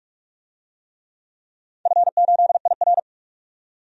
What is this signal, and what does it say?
Morse code, heard as a keyed tone, about 47 wpm, 700 Hz. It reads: F8IR
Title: DXCC-100 Call Signs - 4 Characters - 45wpm